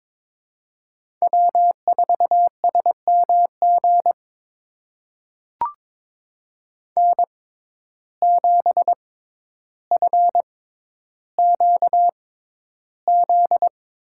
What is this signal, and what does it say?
Morse code, heard as a keyed tone, about 21 wpm, spaced as for 11 wpm, 700 Hz. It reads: W4SMG N7FQZ
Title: USA Call Signs - 5 Characters - 6x Character Spacing - 22wpm